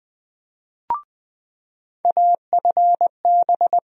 Morse code, heard as a keyed tone, about 20 wpm, 700 Hz. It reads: AFB
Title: USA Call Sign Postfixes - 3 Letters 20wpm